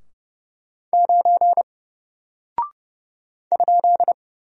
Morse code, heard as a keyed tone, about 30 wpm, 700 Hz. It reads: 9 ?
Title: Single Character (Letter:Number:Punctuation:Prosign) 30wpm